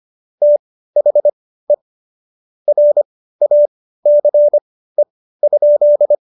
Morse code, heard as a keyed tone, about 25 wpm, spaced as for 15 wpm, 590 Hz. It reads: THE RACE?
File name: Jokes-25-char-15-eff